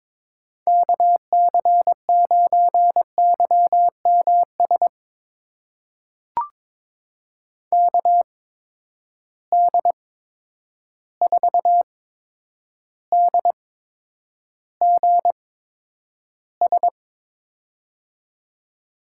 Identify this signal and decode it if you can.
Morse code, heard as a keyed tone, about 22 wpm, spaced as for 9 wpm, 700 Hz. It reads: KC9YMS KD4DGS
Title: USA Call Signs - 8x Character Spacing - 22wpm